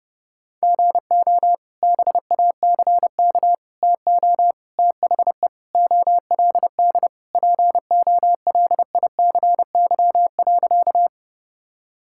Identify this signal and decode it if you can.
Morse code, heard as a keyed tone, about 30 wpm, 700 Hz. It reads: GO BACK TO THE OLD POLICY.
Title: Sentences from Top 1000 Words - Review Part 5 - 30wpm